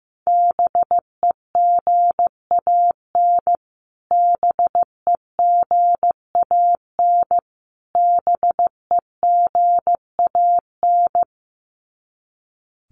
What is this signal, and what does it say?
Morse code, heard as a keyed tone, about 15 wpm, 700 Hz. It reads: BEGAN BEGAN BEGAN
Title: Top 500 Words Repeated 15wpm